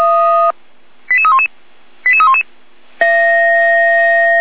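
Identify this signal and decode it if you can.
Five-tone selective call (ZVEI), heard as quick runs of five tones, tones of about 70 ms. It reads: 88311 88311
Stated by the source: ZVEI five-tone selective call received by SDR.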